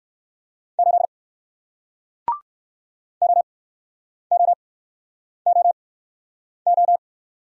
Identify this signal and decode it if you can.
Morse code, heard as a keyed tone, about 59 wpm, spaced as for 9 wpm, 700 Hz. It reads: 8 OOOO